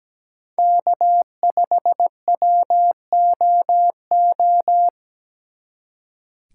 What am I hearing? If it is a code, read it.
Morse code, heard as a keyed tone, about 19 wpm, 700 Hz. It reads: K5WOO
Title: Speed Racing - USA 1x3 Call Signs 17wpm